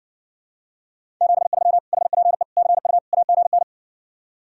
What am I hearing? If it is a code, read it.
Morse code, heard as a keyed tone, about 69 wpm, 700 Hz. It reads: 73 HPE CU AGN